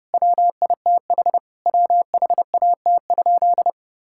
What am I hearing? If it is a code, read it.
Morse code, heard as a keyed tone, about 30 wpm, 700 Hz. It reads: WITH WHAT?
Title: Sentences from Top 1600 Words 30wpm